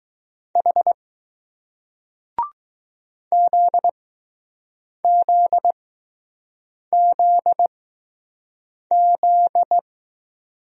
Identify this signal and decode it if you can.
Morse code, heard as a keyed tone, about 20 wpm, spaced as for 6 wpm, 700 Hz. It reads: H ZZZZ